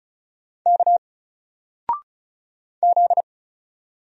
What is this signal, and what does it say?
Morse code, heard as a keyed tone, about 36 wpm, 700 Hz. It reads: K Z
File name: Single-Letter-Number-35wpm